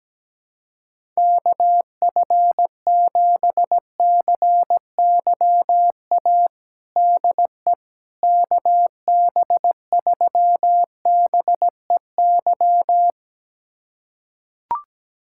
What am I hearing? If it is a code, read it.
Morse code, heard as a keyed tone, about 17 wpm, 700 Hz. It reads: KF7CYA DE KB3BEY
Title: USA Callsign DE Callsign - 6 Characters - 17wpm